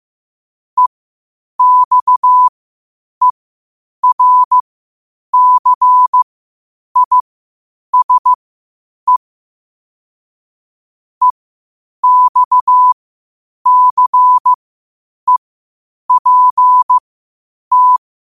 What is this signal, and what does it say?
Morse code, heard as a keyed tone, about 15 wpm, spaced as for 8 wpm, 1000 Hz. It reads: EXERCISE EXCEPT
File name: qotdfiles / 8wpm